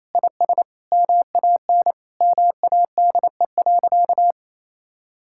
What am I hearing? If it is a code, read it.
Morse code, heard as a keyed tone, about 27 wpm, 700 Hz. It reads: IS MAN MADE.
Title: Sentences from Top 1000 Words - Review Part 6 - 28wpm